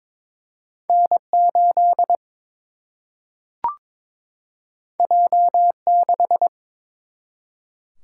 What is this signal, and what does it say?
Morse code, heard as a keyed tone, about 22 wpm, 700 Hz. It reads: N 8 J 6
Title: Single-Letter Single-Number 22wpm